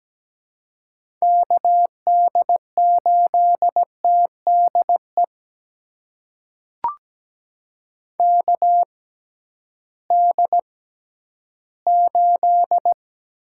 Morse code, heard as a keyed tone, about 17 wpm, spaced as for 11 wpm, 700 Hz. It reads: KD8TDE KD8
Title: USA Call Signs - 6 Characters - 6x Character Spacing - 17wpm